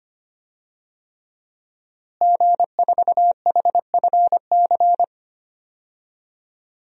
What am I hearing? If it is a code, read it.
Morse code, heard as a keyed tone, about 25 wpm, 700 Hz. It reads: G4HFC
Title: DXCC-100 Call Signs - 5 Characters - Repeated 3x - 25wpm